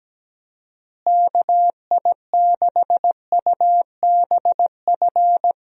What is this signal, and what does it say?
Morse code, heard as a keyed tone, about 17 wpm, 700 Hz. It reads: KI6UBF
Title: USA Call Signs - 6 Characters - Repeated 2x - 17wpm